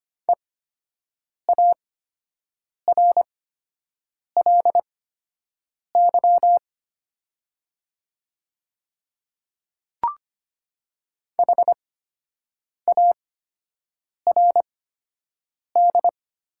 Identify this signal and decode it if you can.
Morse code, heard as a keyed tone, about 25 wpm, spaced as for 7 wpm, 700 Hz. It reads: EARLY HARD
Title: 8x Character Spacing - Top 500 Words - 25wpm